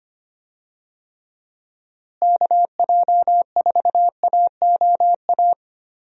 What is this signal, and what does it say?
Morse code, heard as a keyed tone, about 25 wpm, 700 Hz. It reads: KJ4AOA
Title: USA Call Signs - 6 Characters - Repeated 2x - 25wpm